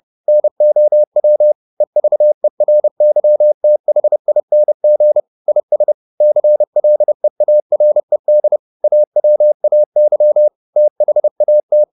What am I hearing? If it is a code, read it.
Morse code, heard as a keyed tone, about 30 wpm, 600 Hz. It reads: NOW EVERYTHING IS CLEARED AWAY THAT
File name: the old man and the sea / the old man and the sea 1 30wpm